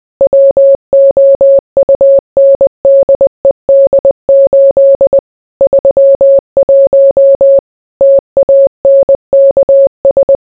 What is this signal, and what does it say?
Morse code, heard as a keyed tone, about 20 wpm, 550 Hz. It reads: WOUNDED 8 31 TANKS